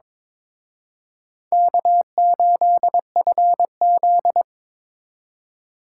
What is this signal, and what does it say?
Morse code, heard as a keyed tone, about 26 wpm, 700 Hz. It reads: K8FZ
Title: Speed Racing - USA 1x2 Call Signs 22wpm